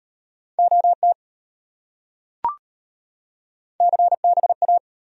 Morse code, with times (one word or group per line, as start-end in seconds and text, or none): 0.6-1.1 OT
3.8-4.8 CBA